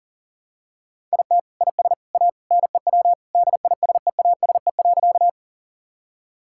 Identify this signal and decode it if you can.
Morse code, heard as a keyed tone, about 40 wpm, 700 Hz. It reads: IT IS A NEW DISEASE.
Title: Sentences from Top 1000 Words - Review Part 6 - 40wpm